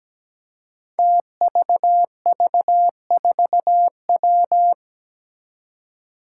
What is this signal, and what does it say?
Morse code, heard as a keyed tone, about 17 wpm, 700 Hz. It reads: TVV4W
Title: Five Challenging Letter-Number Combinations 17wpm